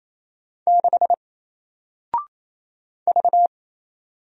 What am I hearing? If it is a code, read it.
Morse code, heard as a keyed tone, about 28 wpm, 700 Hz. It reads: 6 V
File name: Single-Letter-Number-28wpm